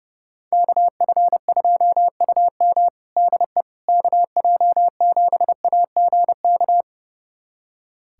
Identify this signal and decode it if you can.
Morse code, heard as a keyed tone, about 30 wpm, 700 Hz. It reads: KF2UM DE KJ7AGK
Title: USA Callsign DE Callsign 30wpm